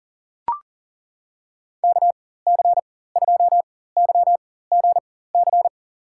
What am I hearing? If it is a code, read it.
Morse code, heard as a keyed tone, about 38 wpm, spaced as for 18 wpm, 700 Hz. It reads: KC2YGC
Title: USA Call Signs - 6 Characters - 4x Character Spacing - 40wpm